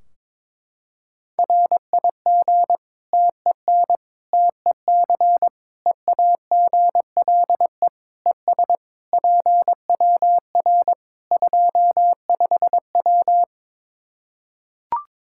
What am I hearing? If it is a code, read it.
Morse code, heard as a keyed tone, about 22 wpm, 700 Hz. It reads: RIG TEN TEC EAGLE ES PWR 25W
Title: K1USN SST QSO Practice 22wpm